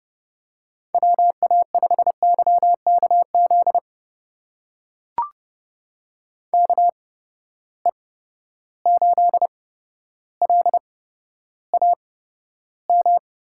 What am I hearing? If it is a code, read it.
Morse code, heard as a keyed tone, about 32 wpm, spaced as for 13 wpm, 700 Hz. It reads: WA5YKZ KE8LAM
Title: USA Call Signs - 6 Characters - 8x Character Spacing - 30wpm